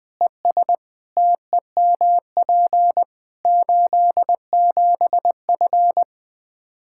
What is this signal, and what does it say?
Morse code, heard as a keyed tone, about 20 wpm, 700 Hz. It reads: ES TEMP 87F
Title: Practice QSO - Exchange 2 - Info - 20wpm